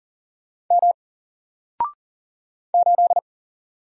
Morse code, heard as a keyed tone, about 39 wpm, 700 Hz. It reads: M 8
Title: Single Letter-Number Alt 40wpm